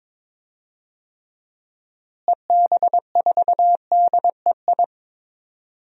Morse code, heard as a keyed tone, about 22 wpm, 700 Hz. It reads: EB4DEI
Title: DXCC-100 Call Signs - 6 Characters - Repeated 3x - 22wpm